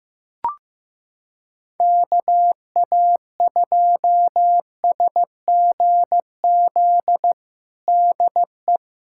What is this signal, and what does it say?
Morse code, heard as a keyed tone, about 15 wpm, 700 Hz. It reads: KA2SGZ DE
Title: USA Callsign DE Callsign 15wpm